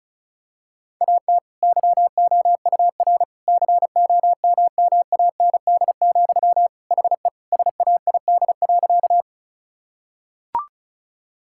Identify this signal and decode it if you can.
Morse code, heard as a keyed tone, about 36 wpm, 700 Hz. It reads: AT YOUR COMMAND, HE SAID.
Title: Sentences from Top 1600 Words 35wpm